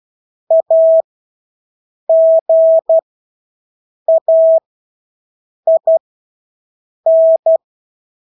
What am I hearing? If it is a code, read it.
Morse code, heard as a keyed tone, about 12 wpm, spaced as for 6 wpm, 650 Hz. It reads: AGAIN
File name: Engelske ord 12-6 wpm